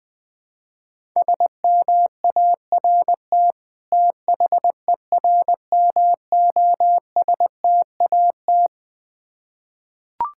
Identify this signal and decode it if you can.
Morse code, heard as a keyed tone, about 21 wpm, 700 Hz. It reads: SMART THERMOSTAT
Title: Difficult Words with Hints - 20wpm